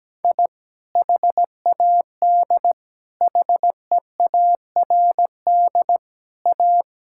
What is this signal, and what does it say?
Morse code, heard as a keyed tone, about 17 wpm, 700 Hz. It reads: I HAD HEARD A
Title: Sentences from Top 1000 Words - Review Part 6 - 17wpm